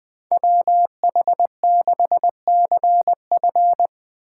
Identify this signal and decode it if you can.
Morse code, heard as a keyed tone, about 20 wpm, 700 Hz. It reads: WH6CF
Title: USA Call Signs - 5 Characters - 3x Character Spacing - 20wpm